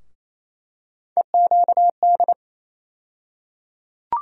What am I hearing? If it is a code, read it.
Morse code, heard as a keyed tone, about 28 wpm, 700 Hz. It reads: EQD
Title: USA Call Sign Postfixes - 3 Letters 28wpm